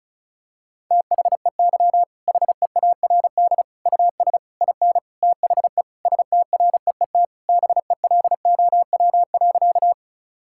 Morse code, heard as a keyed tone, about 34 wpm, 700 Hz. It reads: THEY HEARD US IN THE STREET BELOW.
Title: Sentences from Top 1000 Words - Review Part 5 - 35wpm